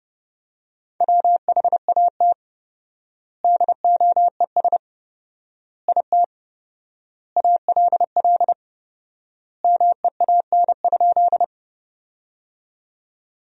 Morse code, heard as a keyed tone, about 30 wpm, 700 Hz. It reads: WHAT DOES IT ALL MEAN?
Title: 4x Word-Spacing - Sentences from Top 200 Words - 30wpm